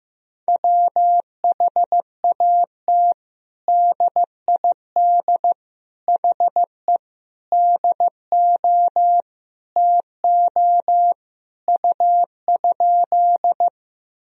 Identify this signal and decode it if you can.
Morse code, heard as a keyed tone, about 15 wpm, 700 Hz. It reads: WHAT DID HE DO TO U?
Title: Sentences from Top 100 Words and Abbreviations 15wpm